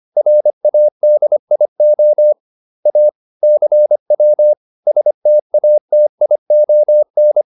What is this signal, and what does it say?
Morse code, heard as a keyed tone, about 25 wpm, 600 Hz. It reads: RADIO A CW STATION